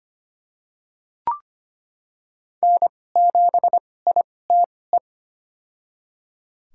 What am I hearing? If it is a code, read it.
Morse code, heard as a keyed tone, about 25 wpm, spaced as for 18 wpm, 700 Hz. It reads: N7ITE
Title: USA Call Signs - 5 Characters - 2x Character Spacing - 25wpm